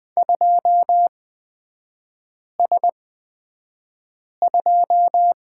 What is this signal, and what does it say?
Morse code, heard as a keyed tone, about 20 wpm, spaced as for 5 wpm, 700 Hz. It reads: 2S2
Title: 243 Lesson 39e - Letter z - Three Characters 20wpm